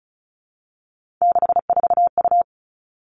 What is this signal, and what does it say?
Morse code, heard as a keyed tone, about 36 wpm, 700 Hz. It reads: 64U
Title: Three Challenging Letter-Number Combinations 35wpm